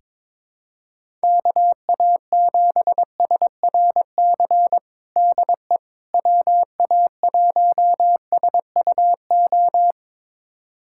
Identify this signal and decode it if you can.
Morse code, heard as a keyed tone, about 22 wpm, 700 Hz. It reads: KA7SRC DE WA1SUO
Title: USA Callsign DE Callsign - 6 Characters - 22wpm